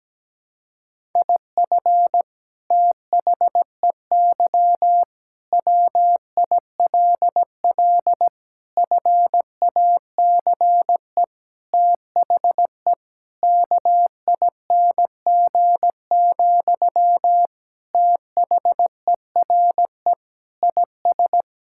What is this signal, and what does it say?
Morse code, heard as a keyed tone, about 17 wpm, 700 Hz. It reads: IF THEY WILL FACE THE KING, THERE IS
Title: Sentences from Top 1000 Words - Review Part 6 - 17wpm